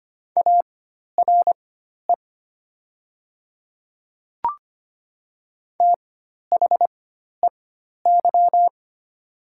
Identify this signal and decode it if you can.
Morse code, heard as a keyed tone, about 25 wpm, spaced as for 12 wpm, 700 Hz. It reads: ARE THEY
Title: 4x Character Spacing - Top 100 Words - 25wpm